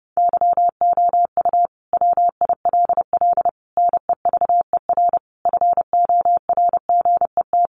Morse code, heard as a keyed tone, about 30 wpm, 700 Hz. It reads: YOU WILL NEVER FORGET
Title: Sentences from Top 1200 Words 30wpm